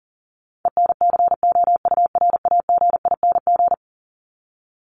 Morse code, heard as a keyed tone, about 40 wpm, 700 Hz. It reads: ENCOURAGING
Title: Top 1751-2000 Words 40wpm